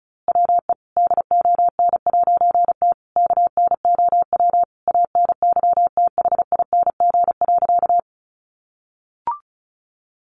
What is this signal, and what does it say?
Morse code, heard as a keyed tone, about 34 wpm, 700 Hz. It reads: WE DON'T KNOW ANYTHING.
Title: A Sound Of Thunder 35wpm